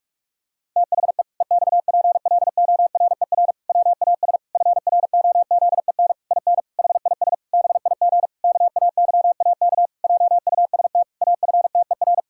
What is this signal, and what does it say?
Morse code, heard as a keyed tone, about 46 wpm, 700 Hz. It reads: THE EXPLORER WAS FROZEN IN HIS BIG KAYAK JUST AFTER
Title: Pangram Challenge v2 - 45wpm